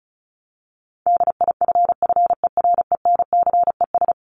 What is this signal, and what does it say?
Morse code, heard as a keyed tone, about 34 wpm, 700 Hz. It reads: DIFFERENCES
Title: ES:S Suffix - Top 100 Words with Base Words 35wpm